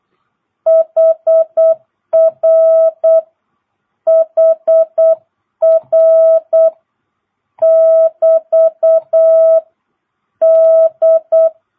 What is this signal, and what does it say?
Morse code, heard as a keyed tone, about 8 wpm, 650 Hz. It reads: HR HR <BT> D
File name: jh8cbh / 2022cw